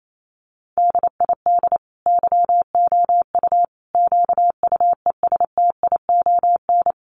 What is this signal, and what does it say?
Morse code, heard as a keyed tone, about 28 wpm, 700 Hz. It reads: DID YOU QUESTION
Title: Sentences from Top 800 Words 28wpm